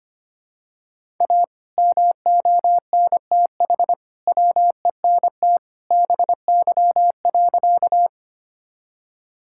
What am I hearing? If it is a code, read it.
Morse code, heard as a keyed tone, about 25 wpm, 700 Hz. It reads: A MONTH WENT BY.
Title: Sentences from Top 1000 Words - Review Part 5 - 25wpm